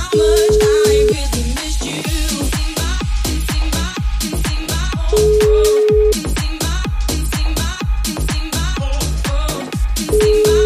Ringback tone (Europe, 425 Hz), with noise.